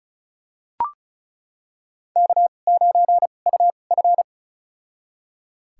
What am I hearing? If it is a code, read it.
Morse code, heard as a keyed tone, about 34 wpm, spaced as for 25 wpm, 700 Hz. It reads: K9UF
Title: USA Call Signs - 2x Character Spacing - 35wpm